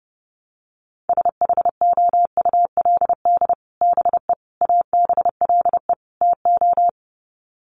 Morse code, heard as a keyed tone, about 30 wpm, 700 Hz. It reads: SHOULD BE ABLE TO
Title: Sets of 4 Words 30wpm